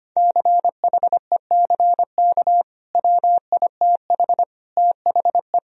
Morse code, heard as a keyed tone, about 25 wpm, 700 Hz. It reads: CHECK WITH THE
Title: Sentences from Top 1000 Words - Review Part 3 - 25wpm